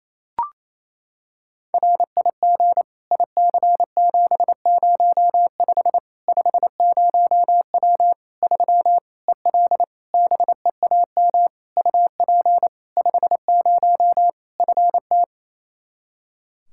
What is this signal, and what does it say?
Morse code, heard as a keyed tone, about 28 wpm, 700 Hz. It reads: RIG IC705 50W 3 EL BEAM UP 50 FT
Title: Practice QSO - Exchange 2 - Info - 28wpm